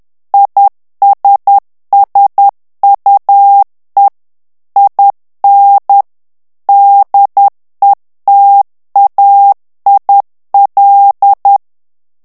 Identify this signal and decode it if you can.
Morse code, heard as a keyed tone, about 10 wpm, 800 Hz. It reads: ISSUE IN DETAIL
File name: ANGLAIS texte clair 3 vitesse 10 mots minute